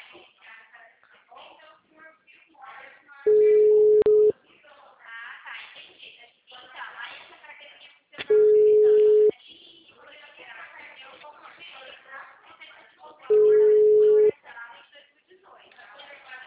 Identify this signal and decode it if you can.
Ringback tone (Europe, 425 Hz), with speech.